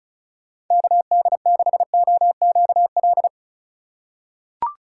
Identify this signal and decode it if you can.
Morse code, heard as a keyed tone, about 36 wpm, 700 Hz. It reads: KD6OQL
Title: USA Call Signs - 6 Characters - 3x Character Spacing - 35wpm